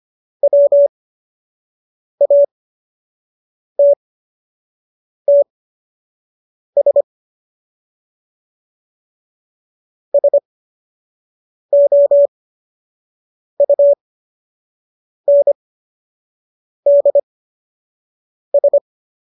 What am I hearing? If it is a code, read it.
Morse code, heard as a keyed tone, about 25 wpm, spaced as for 6 wpm, 570 Hz. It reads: WATTS SOUNDS